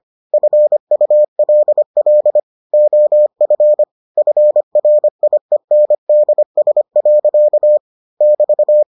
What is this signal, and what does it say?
Morse code, heard as a keyed tone, about 25 wpm, 600 Hz. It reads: FULL OF FRIENDS. <BT>